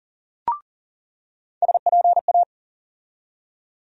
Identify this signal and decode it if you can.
Morse code, heard as a keyed tone, about 40 wpm, 700 Hz. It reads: SPA
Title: USA Call Sign Postfixes - 3 Letters 40wpm